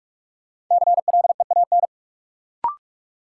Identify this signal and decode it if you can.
Morse code, heard as a keyed tone, about 45 wpm, 700 Hz. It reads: CLEAN
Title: Sentences from Top 1100 Words - New Word Drill - 45wpm